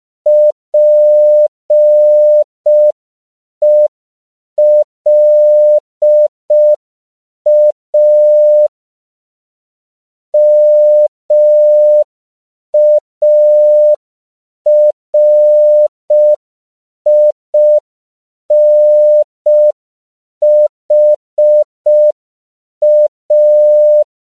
Morse code, heard as a keyed tone, about 5 wpm, 600 Hz. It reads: PELA MARINHA
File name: CW / 5PPM